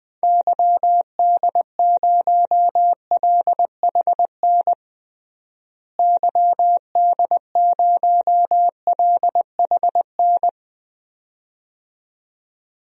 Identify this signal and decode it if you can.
Morse code, heard as a keyed tone, about 20 wpm, 700 Hz. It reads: YD0LHN YD0LHN
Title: DXCC-100 Call Signs - 6 Characters - Repeated 2x - 20wpm